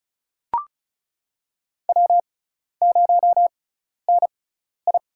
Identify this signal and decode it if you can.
Morse code, heard as a keyed tone, about 34 wpm, spaced as for 12 wpm, 700 Hz. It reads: W0NI
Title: USA Call Signs - 4 Characters - 6x Character Spacing - 35wpm